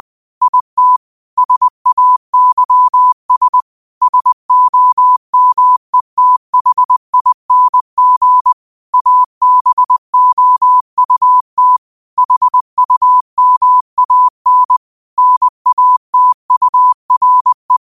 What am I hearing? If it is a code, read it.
Morse code, heard as a keyed tone, about 20 wpm, 1000 Hz. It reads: IT SAYS SOMETHING ABOUT HUMAN NATURE
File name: qotdfiles / 20wpm